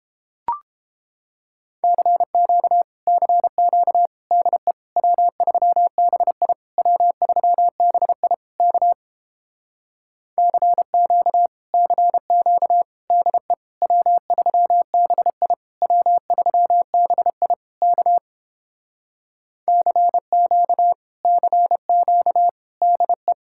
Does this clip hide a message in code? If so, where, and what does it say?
Yes: Morse code from 1.8 to 23.3 s, heard as a keyed tone, about 31 wpm, 700 Hz. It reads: CQ CQ DE W3BI W3BI K CQ CQ DE W3BI W3BI K CQ CQ DE